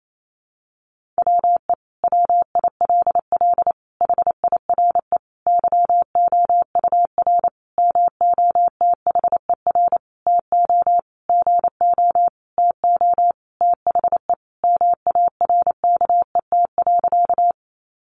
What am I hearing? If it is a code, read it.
Morse code, heard as a keyed tone, about 28 wpm, 700 Hz. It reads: WE WILL HIRE YOUR MOTHER TO GO TO THE MARKET.
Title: Sentences from Top 1400 Words 28wpm